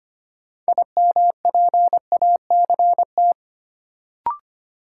Morse code, heard as a keyed tone, about 25 wpm, 700 Hz. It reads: IMPACT
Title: Sentences from Top 1000 Words - Review Part 6 - New Word Drill - 25wpm